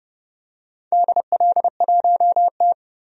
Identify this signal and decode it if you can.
Morse code, heard as a keyed tone, about 30 wpm, 700 Hz. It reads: DL1T
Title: DXCC-100 Call Signs - 4 Characters - 30wpm